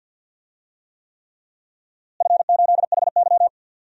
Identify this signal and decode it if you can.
Morse code, heard as a keyed tone, about 52 wpm, 700 Hz. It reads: F8HY
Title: DXCC-100 Call Signs - 4 Characters - Repeated 3x - 50wpm